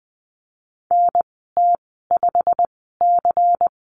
Morse code, heard as a keyed tone, about 20 wpm, spaced as for 15 wpm, 700 Hz. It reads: NT5C
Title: 122 Lesson 20g - Number 3 - Call Signs 20wpm